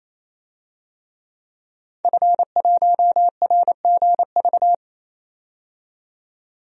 Morse code, heard as a keyed tone, about 28 wpm, 700 Hz. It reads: F1RGV
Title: DXCC-100 Call Signs - 5 Characters - 28wpm